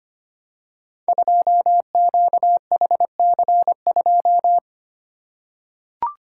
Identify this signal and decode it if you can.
Morse code, heard as a keyed tone, about 25 wpm, 700 Hz. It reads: 2QHC2
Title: Code Groups - 5 Letter:Numbers - 25wpm